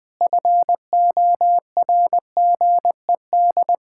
Morse code, heard as a keyed tone, about 20 wpm, 700 Hz. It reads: FORGED
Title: Pangram Challenge v2 - 20wpm